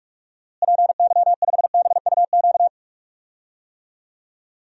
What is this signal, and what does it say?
Morse code, heard as a keyed tone, about 41 wpm, 700 Hz. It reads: PY5BUQ
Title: DXCC-100 Call Signs - 6 Characters - Repeated 3x - 45wpm